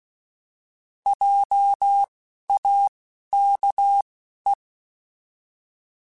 Morse code, heard as a keyed tone, about 16 wpm, spaced as for 11 wpm, 800 Hz. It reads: JAKE
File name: Alphabetical / Alphabetical mp3s / Js @10